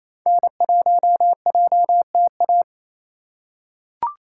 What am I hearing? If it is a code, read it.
Morse code, heard as a keyed tone, about 27 wpm, 700 Hz. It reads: N1JTA
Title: USA Call Signs - 5 Characters - 2x Character Spacing - 28wpm